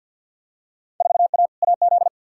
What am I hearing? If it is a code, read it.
Morse code, heard as a keyed tone, about 50 wpm, 700 Hz. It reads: 4A AZ
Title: Field Day Exchanges Encore - 50wpm